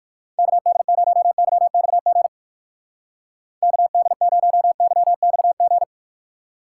Morse code, heard as a keyed tone, about 55 wpm, 700 Hz. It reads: KD0YXG KD0YXG
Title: Speed Racing - USA 2x3 Call Signs 45wpm